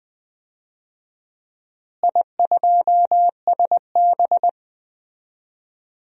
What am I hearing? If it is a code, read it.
Morse code, heard as a keyed tone, about 20 wpm, 700 Hz. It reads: I2SB
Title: DXCC-100 Call Signs - 4 Characters - Repeated 3x - 20wpm